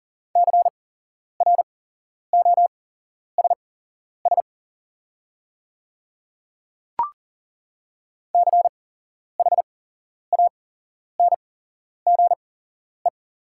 Morse code, heard as a keyed tone, about 41 wpm, spaced as for 11 wpm, 700 Hz. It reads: CROSS CHANGE
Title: 8x Character Spacing - Top 300 Words - 40wpm